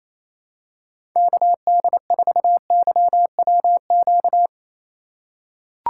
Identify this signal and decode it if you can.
Morse code, heard as a keyed tone, about 28 wpm, 700 Hz. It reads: KD4YWQ
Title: USA Call Signs - 6 Characters - 2x Character Spacing - 28wpm